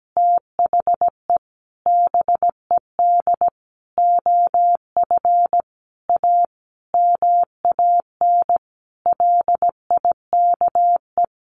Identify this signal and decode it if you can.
Morse code, heard as a keyed tone, about 17 wpm, 700 Hz. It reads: THE BED OF A MAN LIKE
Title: Sentences from Top 1200 Words 17wpm